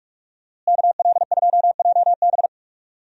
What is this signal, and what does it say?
Morse code, heard as a keyed tone, about 46 wpm, 700 Hz. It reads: KL2JB
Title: USA Call Signs - 5 Characters - Repeated 3x - 45wpm